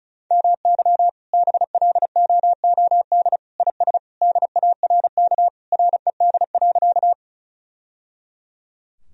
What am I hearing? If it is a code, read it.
Morse code, heard as a keyed tone, about 36 wpm, 700 Hz. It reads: MY BLOOD IS DARK RED.
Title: Sentences from Top 1000 Words - Review Part 2 - 35wpm